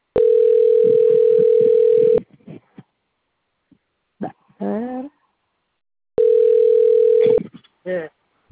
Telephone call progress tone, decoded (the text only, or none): ringback tone